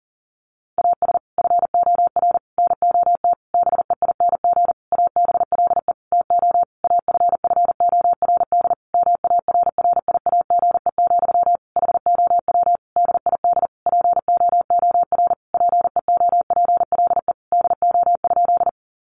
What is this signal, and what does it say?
Morse code, heard as a keyed tone, about 40 wpm, 700 Hz. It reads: AS FOR NOT BEING ABLE TO AFFORD MARRIAGE, HOW DID POOR PEOPLE DO?
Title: Sentences from Top 1100 Words 40wpm